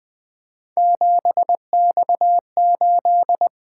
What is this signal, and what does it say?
Morse code, heard as a keyed tone, about 20 wpm, 700 Hz. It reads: 7X8
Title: Three Letter-Number Combinations - 20wpm